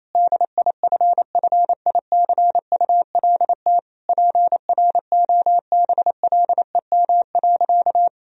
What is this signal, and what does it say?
Morse code, heard as a keyed tone, about 28 wpm, 700 Hz. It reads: DIFFICULT PROBLEM.
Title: Sentences from Top 1000 Words - Review Part 2 - 28wpm